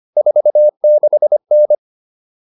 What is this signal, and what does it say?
Morse code, heard as a keyed tone, about 25 wpm, 600 Hz. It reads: 46N